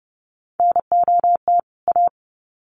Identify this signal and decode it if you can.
Morse code, heard as a keyed tone, about 30 wpm, 700 Hz. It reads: NOT A
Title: Sets of 2 Words 30wpm